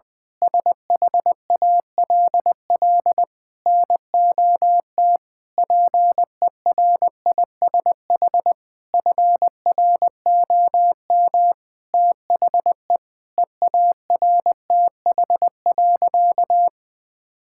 Morse code, heard as a keyed tone, about 20 wpm, 700 Hz. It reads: SHALL NOT PERISH FROM THE EARTH.
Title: Sentences with 9-Word Phrases - Spoken Beginning - 20wpm